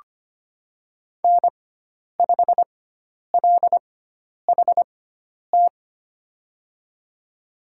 Morse code, heard as a keyed tone, about 25 wpm, spaced as for 10 wpm, 700 Hz. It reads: N5LHT
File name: Lesson 16g - Letter c - Call Signs 25wpm